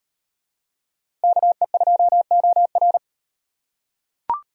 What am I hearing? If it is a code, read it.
Morse code, heard as a keyed tone, about 35 wpm, 700 Hz. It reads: KE2OR